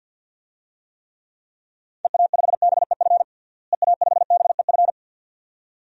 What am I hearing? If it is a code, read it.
Morse code, heard as a keyed tone, about 50 wpm, 700 Hz. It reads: EA5BEF EA5BEF
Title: DXCC-100 Call Signs - 6 Characters - Repeated 2x - 50wpm